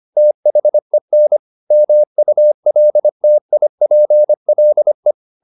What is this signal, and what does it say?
Morse code, heard as a keyed tone, about 25 wpm, 600 Hz. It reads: THEN MULTIPLE